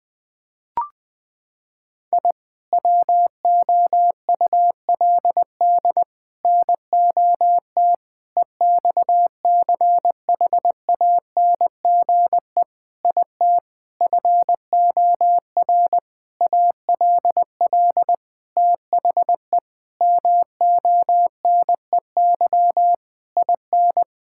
Morse code, heard as a keyed tone, about 21 wpm, 700 Hz. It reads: I WOULD NOT EXCHANGE IT FOR ALL THE MONEY IN
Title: Sentences from Top 1000 Words - Review Part 6 - 20wpm